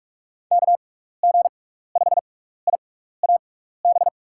Morse code, heard as a keyed tone, about 46 wpm, spaced as for 15 wpm, 700 Hz. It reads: KG5IAB
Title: USA Call Signs - 6 Characters - 6x Character Spacing - 45wpm